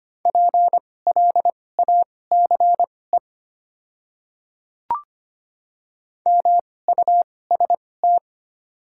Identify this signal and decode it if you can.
Morse code, heard as a keyed tone, about 25 wpm, spaced as for 18 wpm, 690 Hz. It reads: PLACE MUST
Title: 2x Character Spacing - Top 200 Words - 25wpm